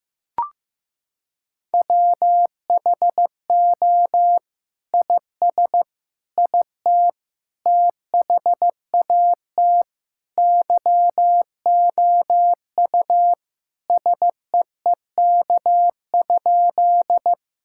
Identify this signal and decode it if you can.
Morse code, heard as a keyed tone, about 15 wpm, 700 Hz. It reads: WHO IS IT THAT YOU SEEK?
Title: Sentences from Top 1600 Words 15wpm